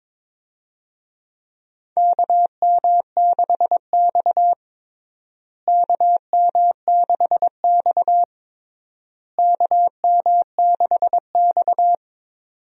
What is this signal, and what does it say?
Morse code, heard as a keyed tone, about 22 wpm, 700 Hz. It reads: KM6X KM6X KM6X
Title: USA Call Signs - 4 Characters - Repeated 3x - 22wpm